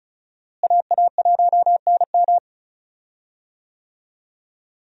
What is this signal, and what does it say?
Morse code, heard as a keyed tone, about 36 wpm, 700 Hz. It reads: AA1NM
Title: USA Call Signs - 5 Characters - Repeated 3x - 35wpm